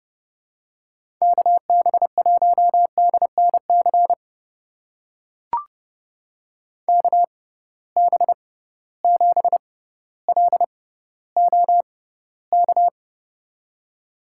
Morse code, heard as a keyed tone, about 31 wpm, spaced as for 16 wpm, 700 Hz. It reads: KB1DNC KB7LOK
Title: USA Call Signs - 6 Characters - 6x Character Spacing - 30wpm